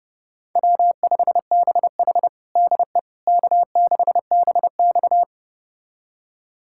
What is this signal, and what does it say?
Morse code, heard as a keyed tone, about 30 wpm, 700 Hz. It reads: W5BH DE K6BX
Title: USA Callsign DE Callsign - 4 Characters - 30wpm